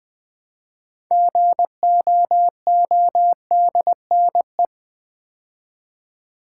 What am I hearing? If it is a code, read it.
Morse code, heard as a keyed tone, about 20 wpm, 700 Hz. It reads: GOODNE
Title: Build A Word - 3 Letters At A Time - 8-Letter Words - 20wpm